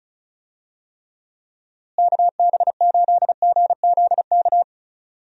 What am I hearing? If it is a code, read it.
Morse code, heard as a keyed tone, about 34 wpm, 700 Hz. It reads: KB8GZK
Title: USA Call Signs - 6 Characters - Repeated 2x - 35wpm